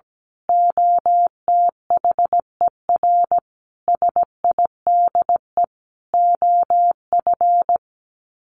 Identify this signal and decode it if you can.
Morse code, heard as a keyed tone, about 17 wpm, 700 Hz. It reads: OTHER SIDE OF
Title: Sets of 5 Words 17wpm